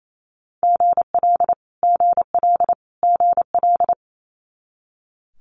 Morse code, heard as a keyed tone, about 28 wpm, 700 Hz. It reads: GL GL GL
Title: Multi-letter Phonemes Repeated 3x 28wpm